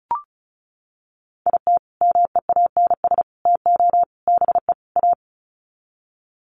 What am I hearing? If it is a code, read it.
Morse code, heard as a keyed tone, about 34 wpm, 700 Hz. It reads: IT MEANS TO BE A
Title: Sets of 5 Words Encore2 35wpm